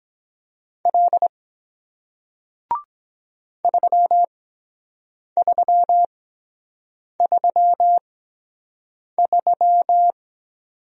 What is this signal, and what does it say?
Morse code, heard as a keyed tone, about 23 wpm, spaced as for 7 wpm, 700 Hz. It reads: L 3333